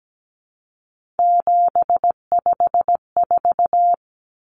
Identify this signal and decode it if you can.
Morse code, heard as a keyed tone, about 17 wpm, 700 Hz. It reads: 754
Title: Common Cut Numbers - 3-Digit 17wpm